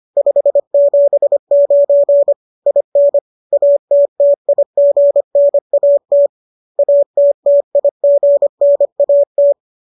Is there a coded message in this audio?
Yes: Morse code from 0.2 to 9.5 s, heard as a keyed tone, about 25 wpm, 570 Hz. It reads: 579 IN ATTIGNAT ATTIGNAT